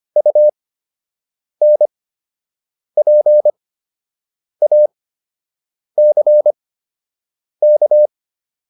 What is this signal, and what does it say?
Morse code, heard as a keyed tone, about 25 wpm, spaced as for 7 wpm, 600 Hz. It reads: UNPACK